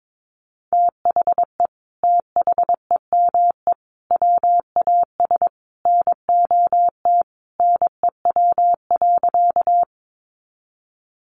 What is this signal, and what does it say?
Morse code, heard as a keyed tone, about 22 wpm, 700 Hz. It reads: THE THEME WAS NOT NEW.
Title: Sentences from Top 1100 Words 22wpm